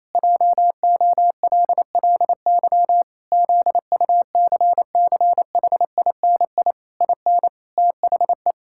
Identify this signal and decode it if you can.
Morse code, heard as a keyed tone, about 28 wpm, 700 Hz. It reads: JOLLY ZUCCHINI IN THE
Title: Pangram Challenge v2 - 28wpm